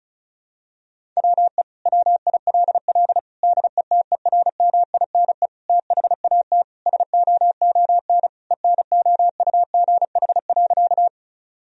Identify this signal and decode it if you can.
Morse code, heard as a keyed tone, about 36 wpm, 700 Hz. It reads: WE WILL DETERMINE THAT SOON ENOUGH.
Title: Sentences from Top 1000 Words - Review Part 4 - 35wpm